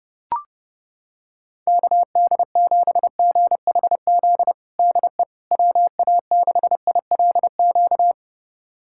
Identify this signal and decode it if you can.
Morse code, heard as a keyed tone, about 32 wpm, 700 Hz. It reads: KD7GHZ DE WA6ILQ
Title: USA Callsign DE Callsign - 6 Characters - 30wpm